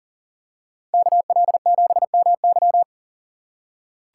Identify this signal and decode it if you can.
Morse code, heard as a keyed tone, about 40 wpm, 700 Hz. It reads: KL7MY
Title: USA Call Signs - 5 Characters - 6x Character Spacing - 40wpm